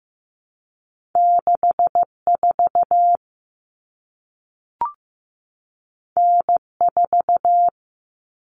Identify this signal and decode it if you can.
Morse code, heard as a keyed tone, about 15 wpm, 700 Hz. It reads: 64 N4
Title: Common Cut Numbers - 2-Digit 15wpm